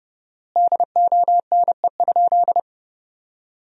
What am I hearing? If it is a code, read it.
Morse code, heard as a keyed tone, about 28 wpm, 700 Hz. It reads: DONE?
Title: 4x Word-Spacing - Sentences from Top 500 Words - 30wpm